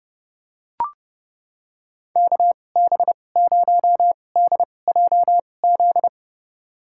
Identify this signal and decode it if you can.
Morse code, heard as a keyed tone, about 30 wpm, spaced as for 22 wpm, 700 Hz. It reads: KB0DJZ
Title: USA Call Signs - 6 Characters - 2x Character Spacing - 30wpm